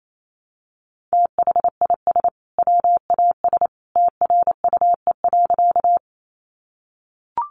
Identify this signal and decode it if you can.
Morse code, heard as a keyed tone, about 28 wpm, 700 Hz. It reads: THIS WAS TRUE.
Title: Sentences from Top 400 Words 28wpm